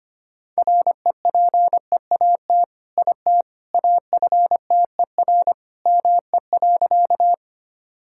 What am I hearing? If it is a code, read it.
Morse code, heard as a keyed tone, about 25 wpm, 700 Hz. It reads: REPEAT IT AFTER ME.
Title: Sentences from Top 1600 Words 25wpm